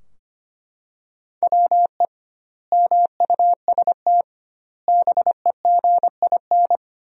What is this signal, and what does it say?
Morse code, heard as a keyed tone, about 25 wpm, 700 Hz. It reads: WE MUST BEGIN
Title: 2x Word-Spacing - Sentences from Top 300 Words - 25wpm